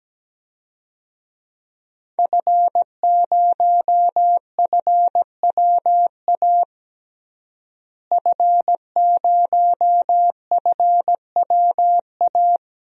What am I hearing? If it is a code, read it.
Morse code, heard as a keyed tone, about 17 wpm, 700 Hz. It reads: F0FWA F0FWA
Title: DXCC-100 Call Signs - 5 Characters - Repeated 2x - 17wpm